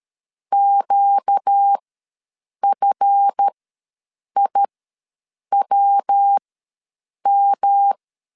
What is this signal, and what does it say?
Morse code, heard as a keyed tone, about 13 wpm, spaced as for 7 wpm, 800 Hz. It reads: QFIWM
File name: LEZIONI / 7 GF9Z / 7 GF ws 12wpm 3140